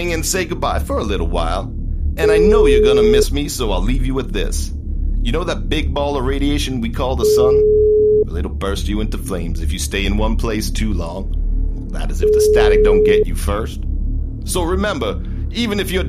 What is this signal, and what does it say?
Ringback tone (Europe, 425 Hz), with speech.